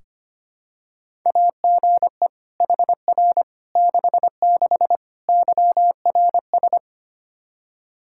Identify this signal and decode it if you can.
Morse code, heard as a keyed tone, about 25 wpm, 700 Hz. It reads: AGE HR 66 YRS
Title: Practice QSO - Exchange 3 - More Info - 25wpm